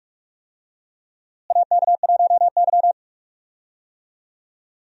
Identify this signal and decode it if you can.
Morse code, heard as a keyed tone, about 41 wpm, 700 Hz. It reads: AK1Y
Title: USA Call Signs - 4 Characters - Repeated 3x - 45wpm